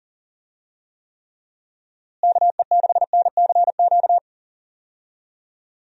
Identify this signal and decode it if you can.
Morse code, heard as a keyed tone, about 40 wpm, 700 Hz. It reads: KE6NCQ
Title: USA Call Signs - 6 Characters - Repeated 2x - 40wpm